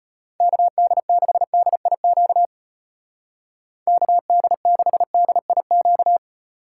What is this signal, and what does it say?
Morse code, heard as a keyed tone, about 34 wpm, 700 Hz. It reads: KD6DIQ KD6DIQ